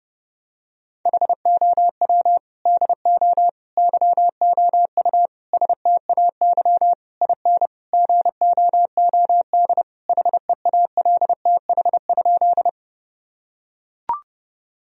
Morse code, heard as a keyed tone, about 30 wpm, 700 Hz. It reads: HOW DO YOU STAY IN GOOD HEALTH?
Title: Sentences from Top 1000 Words - Review Part 3 - 30wpm